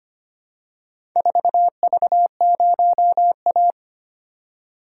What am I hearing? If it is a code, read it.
Morse code, heard as a keyed tone, about 25 wpm, 700 Hz. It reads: 4V0A
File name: practice / 25wpm / Four Letter-Number Combinations - 25wpm